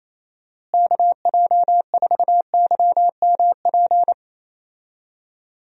Morse code, heard as a keyed tone, about 28 wpm, 700 Hz. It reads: KJ4YMP
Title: USA Call Signs - 6 Characters - Repeated 3x - 28wpm